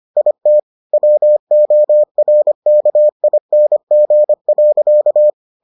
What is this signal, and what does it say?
Morse code, heard as a keyed tone, about 25 wpm, 600 Hz. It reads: IT WORKING.